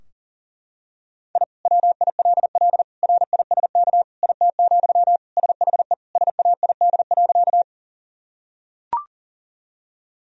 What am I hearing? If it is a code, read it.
Morse code, heard as a keyed tone, about 40 wpm, 700 Hz. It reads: I WILL RISK IT, SHE SAID.
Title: Sentences from Top 1000 Words - Review Part 4 - 40wpm